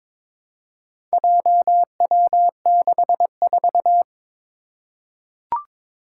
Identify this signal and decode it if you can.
Morse code, heard as a keyed tone, about 21 wpm, 700 Hz. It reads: JW64
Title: Four Challenging Letter-Number Combinations 22wpm